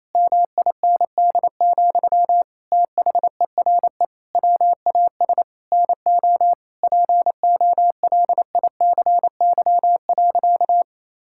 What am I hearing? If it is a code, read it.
Morse code, heard as a keyed tone, about 28 wpm, 700 Hz. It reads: MIND, THERE WAS NO POLICY.
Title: Sentences from Top 1000 Words - Review Part 5 - 28wpm